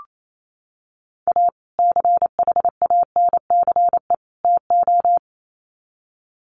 Morse code, heard as a keyed tone, about 28 wpm, 700 Hz. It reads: A CHANCE TO
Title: Sets of 3 Words 28wpm